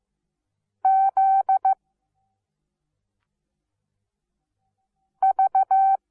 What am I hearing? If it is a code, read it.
Morse code, heard as a keyed tone, about 15 wpm, 770 Hz. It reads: Z V
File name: Onlinelehrgang / morsen / DJ4UF Morsen17